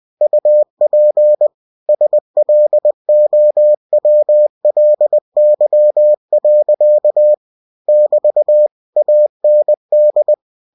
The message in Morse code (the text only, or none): UP SLOWLY. <BT> AND